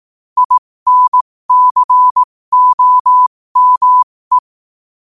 Morse code, heard as a keyed tone, about 18 wpm, spaced as for 15 wpm, 1000 Hz. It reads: INCOME
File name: qotdfiles / 15wpm